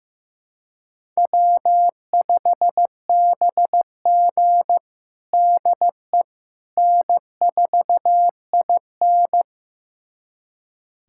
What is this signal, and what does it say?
Morse code, heard as a keyed tone, about 15 wpm, 700 Hz. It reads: W5BG DE N4IN
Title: USA Callsign DE Callsign - 4 Characters - 15wpm